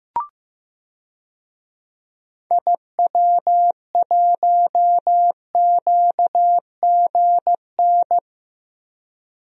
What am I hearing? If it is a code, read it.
Morse code, heard as a keyed tone, about 15 wpm, 700 Hz. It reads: IW1QGN
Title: DXCC-100 Call Signs - 6 Characters - Repeated 2x - 15wpm